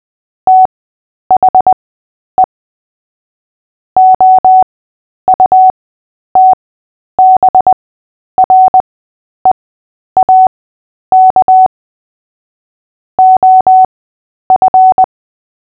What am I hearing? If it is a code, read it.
Morse code, heard as a keyed tone, about 20 wpm, spaced as for 10 wpm, 750 Hz. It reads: THE OUTBREAK OF
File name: Short-Story-7-10wpm